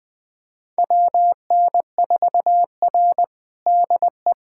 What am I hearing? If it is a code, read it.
Morse code, heard as a keyed tone, about 21 wpm, 700 Hz. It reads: WN4R DE
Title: USA Callsign DE Callsign - 4 Characters - 20wpm